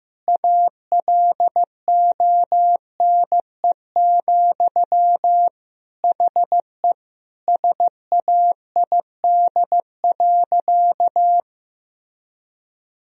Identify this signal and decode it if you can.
Morse code, heard as a keyed tone, about 15 wpm, 700 Hz. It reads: ALONE, HE SAID.
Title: Sentences from Top 1000 Words - Review Part 2 - 15wpm